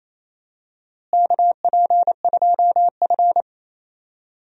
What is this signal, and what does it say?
Morse code, heard as a keyed tone, about 28 wpm, 700 Hz. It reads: KP2F
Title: DXCC-100 Call Signs - 4 Characters - Repeated 3x - 28wpm